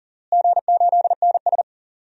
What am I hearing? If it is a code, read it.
Morse code, heard as a keyed tone, about 40 wpm, 690 Hz. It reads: G8NS
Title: DXCC-100 Call Signs - 4 Characters - Repeated 2x - 40wpm